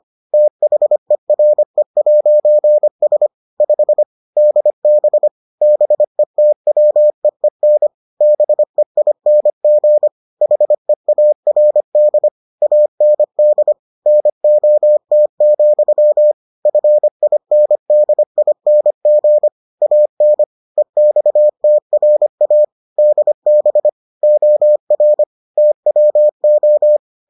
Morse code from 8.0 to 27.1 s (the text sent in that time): BEING HEARD AND NOT, FINDING AN EXTRA DB OR TWO